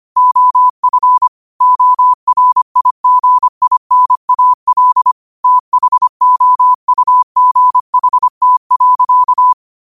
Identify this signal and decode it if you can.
Morse code, heard as a keyed tone, about 25 wpm, 1000 Hz. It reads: OF ORIGINAL THOUGHT.
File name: 25wpm